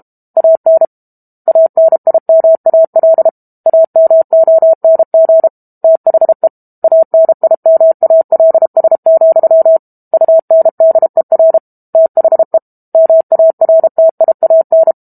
Morse code, heard as a keyed tone, about 32 wpm, 650 Hz. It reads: AN ANIMAL AMONG THE ANIMALS, UNDER THE MARTIAN
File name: Book Two / 06-Chapter 6 - 32 WPM